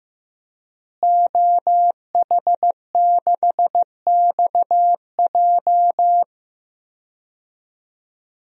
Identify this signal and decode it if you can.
Morse code, heard as a keyed tone, about 15 wpm, 700 Hz. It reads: OH6XJ
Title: DXCC-100 Call Signs - 5 Characters - Repeated 2x - 15wpm